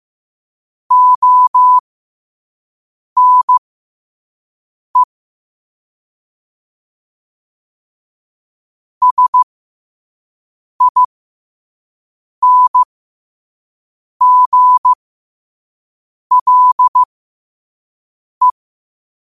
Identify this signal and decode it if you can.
Morse code, heard as a keyed tone, about 15 wpm, spaced as for 5 wpm, 1000 Hz. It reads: ONE SINGLE
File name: qotdfiles / 5wpm